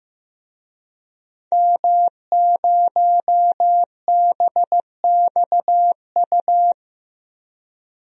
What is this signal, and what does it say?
Morse code, heard as a keyed tone, about 15 wpm, 700 Hz. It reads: M0BXU
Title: DXCC-100 Call Signs - 5 Characters - Repeated 3x - 15wpm